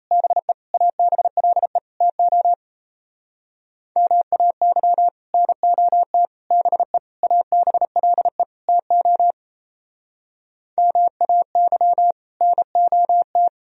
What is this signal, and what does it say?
Morse code, heard as a keyed tone, about 35 wpm, 700 Hz. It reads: BE ABLE TO MAY NOT BE ABLE TO MAY NOT
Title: Speed-Racing - Sets of 5 Words 28wpm